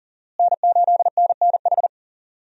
Morse code, heard as a keyed tone, about 40 wpm, 700 Hz. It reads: N8NNH
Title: USA Call Signs - 5 Characters - Repeated 2x - 40wpm